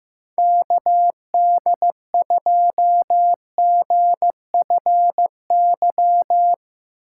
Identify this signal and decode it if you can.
Morse code, heard as a keyed tone, about 15 wpm, 700 Hz. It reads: KD2GFY
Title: USA Call Signs - 6 Characters - 4x Character Spacing - 15wpm